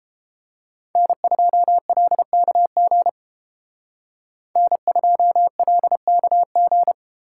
Morse code, heard as a keyed tone, about 30 wpm, 700 Hz. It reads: N2LKG N2LKG